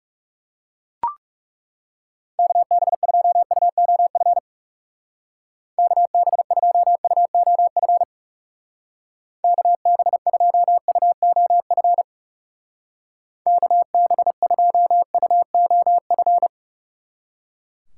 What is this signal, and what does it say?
Morse code, heard as a keyed tone, about 36 wpm, 700 Hz. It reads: KB2UOF KB2UOF KB2UOF KB2UOF
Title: Speed Racing - USA 2x3 Call Signs 30wpm